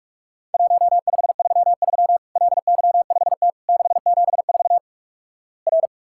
Morse code, heard as a keyed tone, about 47 wpm, 700 Hz. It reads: 1533 LY5T 674 R
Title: Worked All Europe DX Contest - QTC Exchange - 45wpm